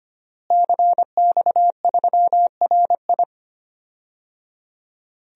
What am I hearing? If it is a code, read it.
Morse code, heard as a keyed tone, about 25 wpm, 700 Hz. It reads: CX3RI
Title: DXCC-100 Call Signs - 5 Characters - Repeated 3x - 25wpm